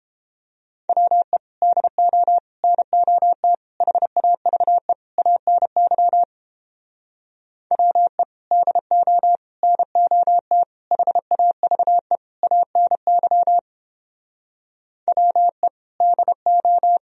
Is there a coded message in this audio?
Yes: Morse code from 0.9 to 17.0 s, heard as a keyed tone, about 31 wpm, 700 Hz. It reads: WE DO NOT HAVE ANY WE DO NOT HAVE ANY WE DO